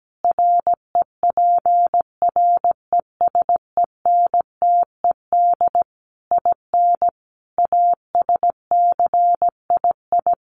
Morse code, heard as a keyed tone, about 17 wpm, 700 Hz. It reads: REPRESENTED IN ASCII
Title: Sets of 5 Words 17wpm